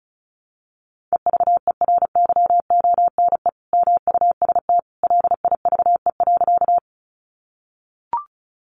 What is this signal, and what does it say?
Morse code, heard as a keyed tone, about 34 wpm, 700 Hz. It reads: EVERYONE MUST LIVE.
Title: Sentences from Top 800 Words 35wpm